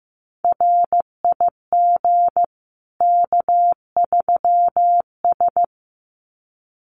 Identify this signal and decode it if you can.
Morse code, heard as a keyed tone, about 15 wpm, 700 Hz. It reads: RIG K3S
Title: Common QSO Phrases 15wpm